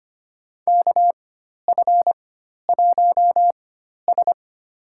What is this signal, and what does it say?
Morse code, heard as a keyed tone, about 25 wpm, spaced as for 12 wpm, 700 Hz. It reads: KF1S
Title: USA Call Signs - 4 Characters - 4x Character Spacing - 25wpm